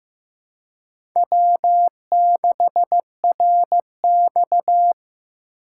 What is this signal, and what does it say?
Morse code, heard as a keyed tone, about 15 wpm, 700 Hz. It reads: W6RX
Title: USA Call Signs - 4 Characters - 3x Character Spacing - 15wpm